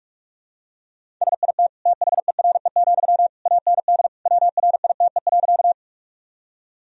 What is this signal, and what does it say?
Morse code, heard as a keyed tone, about 46 wpm, 700 Hz. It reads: SIT THERE, AND WRITE.
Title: Sentences from Top 1000 Words - Review Part 1 - 45wpm